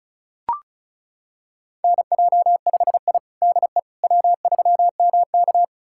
Morse code, heard as a keyed tone, about 34 wpm, 700 Hz. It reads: NJ5I DE W3MK
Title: USA Callsign DE Callsign - 4 Characters - 35wpm